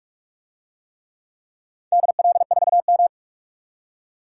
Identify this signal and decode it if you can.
Morse code, heard as a keyed tone, about 44 wpm, 700 Hz. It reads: DL4M
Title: DXCC-100 Call Signs - 4 Characters - 45wpm